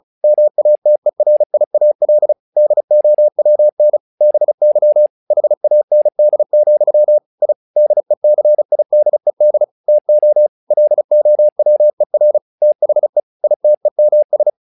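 Morse code, heard as a keyed tone, about 34 wpm, 600 Hz. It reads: MATERIAL DOWN BY HAND, I DECIDED TO LOWER THE ITEMS
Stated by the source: Morse practice text "Tower Trouble" sent at 35 WPM, from the SKCC - Straight Key Century Club learning center audio files.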